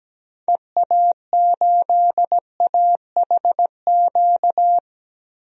Morse code, heard as a keyed tone, about 17 wpm, 700 Hz. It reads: EA8AHQ
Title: DXCC-100 Call Signs - 6 Characters - Repeated 3x - 17wpm